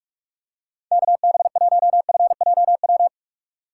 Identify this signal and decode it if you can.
Morse code, heard as a keyed tone, about 45 wpm, 700 Hz. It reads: KB1FJW
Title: USA Call Signs - 6x Character Spacing - 45wpm